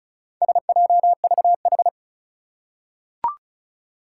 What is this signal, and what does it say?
Morse code, heard as a keyed tone, about 34 wpm, 700 Hz. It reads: SJVH
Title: Four Challenging Letter-Number Combinations 35wpm